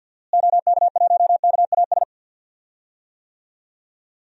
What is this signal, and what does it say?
Morse code, heard as a keyed tone, about 46 wpm, 700 Hz. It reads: OK1KAS
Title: DXCC-100 Call Signs - 6 Characters - Repeated 3x - 50wpm